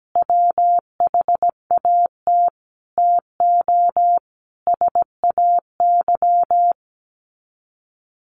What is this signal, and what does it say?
Morse code, heard as a keyed tone, about 17 wpm, 700 Hz. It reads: WHAT TO SAY
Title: Sets of 5 Words 17wpm